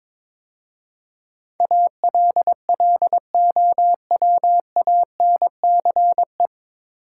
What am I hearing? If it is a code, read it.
Morse code, heard as a keyed tone, about 22 wpm, 700 Hz. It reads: ALLOWANCE
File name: 9 Letter Words - First 3 Letters Given - 22wpm